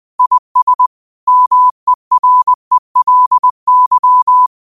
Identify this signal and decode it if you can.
Morse code, heard as a keyed tone, about 20 wpm, 1000 Hz. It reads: IS MERELY